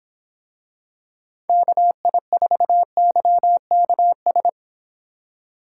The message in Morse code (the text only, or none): KI4YKS